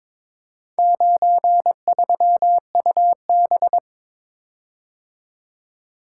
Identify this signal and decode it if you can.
Morse code, heard as a keyed tone, about 22 wpm, 700 Hz. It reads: 93UB
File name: practice / 22wpm / Four Letter-Number Combinations - 22wpm